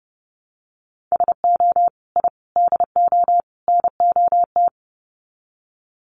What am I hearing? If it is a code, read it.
Morse code, heard as a keyed tone, about 30 wpm, 700 Hz. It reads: SO I DO NOT
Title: Sets of 4 Words 30wpm